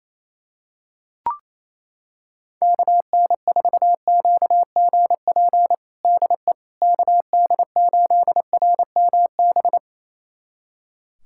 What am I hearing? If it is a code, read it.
Morse code, heard as a keyed tone, about 28 wpm, 700 Hz. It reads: KN4QGP DE KD8RMB
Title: USA Callsign DE Callsign 28wpm